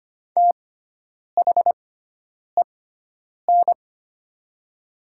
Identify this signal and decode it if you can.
Morse code, heard as a keyed tone, about 25 wpm, spaced as for 9 wpm, 700 Hz. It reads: THEN
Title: 6x Character Spacing - Top 200 Words - 25wpm